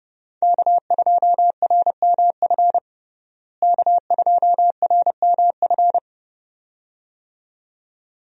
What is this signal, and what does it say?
Morse code, heard as a keyed tone, about 32 wpm, 700 Hz. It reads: K2RMF K2RMF
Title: USA Call Signs - 5 Characters - Repeated 2x - 30wpm